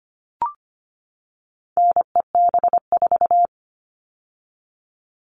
Morse code, heard as a keyed tone, about 25 wpm, 700 Hz. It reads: NEB4
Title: All Cut Numbers - 4-Digit 25wpm